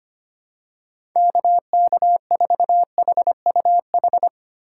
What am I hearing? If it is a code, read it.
Morse code, heard as a keyed tone, about 25 wpm, 700 Hz. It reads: KK4HUH
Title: USA Call Signs - 6 Characters - 4x Character Spacing - 25wpm